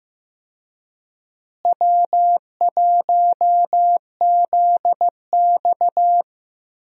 Morse code, heard as a keyed tone, about 15 wpm, 700 Hz. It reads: W1ZX
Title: USA Call Signs - 4 Characters - Repeated 2x - 15wpm